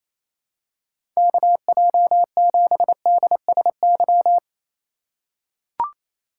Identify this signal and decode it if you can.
Morse code, heard as a keyed tone, about 28 wpm, 700 Hz. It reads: KJ7DSY
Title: USA Call Signs - 2x Character Spacing - 28wpm